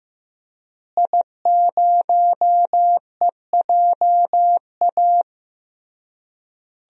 Morse code, heard as a keyed tone, about 15 wpm, 700 Hz. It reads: I0EJA
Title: DXCC-100 Call Signs - 5 Characters - Repeated 3x - 15wpm